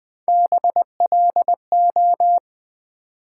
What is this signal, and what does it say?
Morse code, heard as a keyed tone, about 20 wpm, 700 Hz. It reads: BLO
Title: Build A Word - 1 Letter At A Time - 4-Letter Words - 20wpm